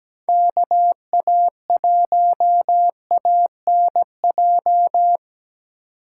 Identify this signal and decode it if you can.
Morse code, heard as a keyed tone, about 20 wpm, 700 Hz. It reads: KA1ANJ
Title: Speed Racing - USA 2x3 Call Signs 17wpm